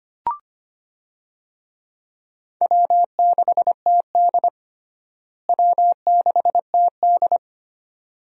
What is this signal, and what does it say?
Morse code, heard as a keyed tone, about 25 wpm, 700 Hz. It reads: W6TD W6TD
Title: USA Call Signs - 4 Characters - Repeated 2x - 25wpm